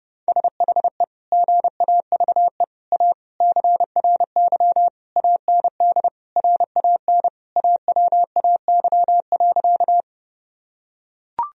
Morse code, heard as a keyed tone, about 30 wpm, 700 Hz. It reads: SHE GAVE A CRY AND RAN AWAY.
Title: Sentences from Top 1000 Words - Review Part 3 - 30wpm